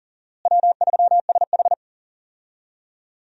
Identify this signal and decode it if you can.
Morse code, heard as a keyed tone, about 40 wpm, 700 Hz. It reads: W3SH
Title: USA Call Signs - 4 Characters - Repeated 3x - 40wpm